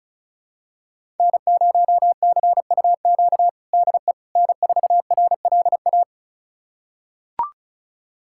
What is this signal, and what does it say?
Morse code, heard as a keyed tone, about 36 wpm, 700 Hz. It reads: N0CUQ DE N4RLA
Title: USA Callsign DE Callsign - 5 Characters - 35wpm